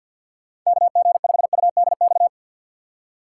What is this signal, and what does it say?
Morse code, heard as a keyed tone, about 48 wpm, 700 Hz. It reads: KG5UDX
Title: USA Call Signs - 6 Characters - 3x Character Spacing - 50wpm